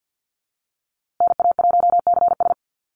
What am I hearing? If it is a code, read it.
Morse code, heard as a keyed tone, about 50 wpm, 700 Hz. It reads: NA1CS
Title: USA Call Signs Repeated 2x 50wpm